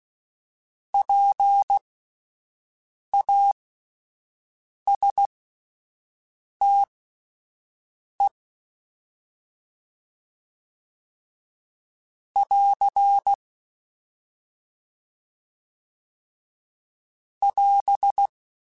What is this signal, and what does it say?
Morse code, heard as a keyed tone, about 16 wpm, spaced as for 5 wpm, 800 Hz. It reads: PASTE <AR> <AS>